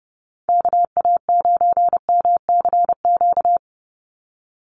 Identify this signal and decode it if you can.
Morse code, heard as a keyed tone, about 30 wpm, 700 Hz. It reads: KA9MCQ
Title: USA Call Signs 30wpm